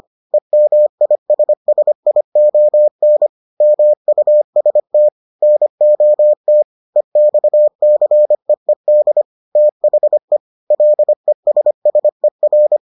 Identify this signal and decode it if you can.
Morse code, heard as a keyed tone, about 25 wpm, 600 Hz. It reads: EMISSION MUST NOT EXCEED THE LESSER